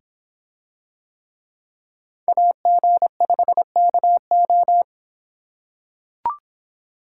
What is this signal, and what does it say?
Morse code, heard as a keyed tone, about 26 wpm, 700 Hz. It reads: AG5KO